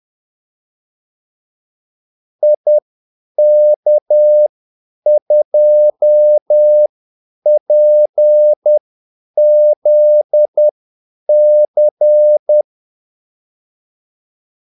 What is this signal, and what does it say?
Morse code, heard as a keyed tone, about 10 wpm, spaced as for 8 wpm, 600 Hz. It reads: IK2PZC
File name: Callsigns-10-8wpm